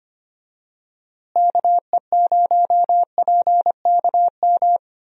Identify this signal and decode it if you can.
Morse code, heard as a keyed tone, about 25 wpm, 700 Hz. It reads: KE0PKM
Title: USA Call Signs - 6 Characters - Repeated 3x - 25wpm